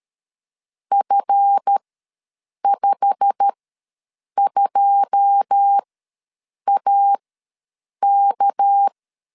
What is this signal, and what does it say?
Morse code, heard as a keyed tone, about 13 wpm, spaced as for 7 wpm, 800 Hz. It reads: F52AK
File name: LEZIONI / 9 28 ws 12wpm 3354